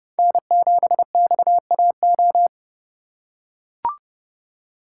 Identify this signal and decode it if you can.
Morse code, heard as a keyed tone, about 28 wpm, 700 Hz. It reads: N7XAO